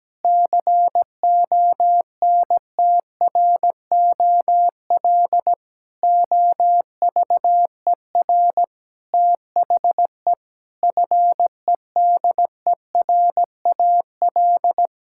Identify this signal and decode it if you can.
Morse code, heard as a keyed tone, about 17 wpm, 700 Hz. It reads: CONTROL OVER THE FEDERAL
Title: Sentences from Top 1000 Words - Review Part 5 - 17wpm